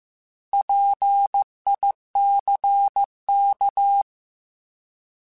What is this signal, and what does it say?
Morse code, heard as a keyed tone, about 15 wpm, 800 Hz. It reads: PICK
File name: Alphabetical / Alphabetical mp3s / Ps @13